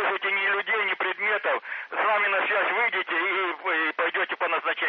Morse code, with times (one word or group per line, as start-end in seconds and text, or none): none